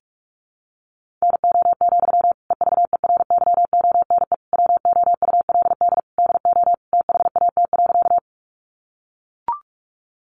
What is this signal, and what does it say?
Morse code, heard as a keyed tone, about 44 wpm, 700 Hz. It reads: NO, EVERYONE WOULD DO THAT.
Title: Sentences from Top 800 Words 45wpm